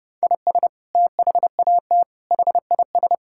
Morse code, heard as a keyed tone, about 30 wpm, 700 Hz. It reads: IS THAT HIS
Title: Sentences from Top 1000 Words - Review Part 5 - 30wpm